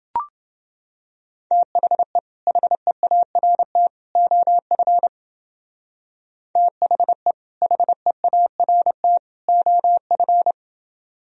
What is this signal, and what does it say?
Morse code, heard as a keyed tone, about 29 wpm, 700 Hz. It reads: THE HEART OF THE HEART OF